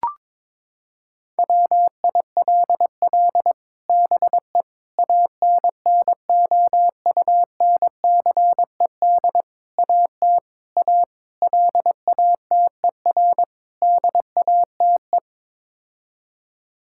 Morse code, heard as a keyed tone, about 22 wpm, 700 Hz. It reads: WILL BE ANNOUNCED AT A LATER DATE
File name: practice / 22wpm / Sets of 7 Words - 22wpm